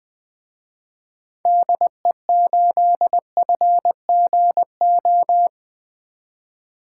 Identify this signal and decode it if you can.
Morse code, heard as a keyed tone, about 20 wpm, 700 Hz. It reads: DE8FGO
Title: Speed Racing - USA 2x3 Call Signs 17wpm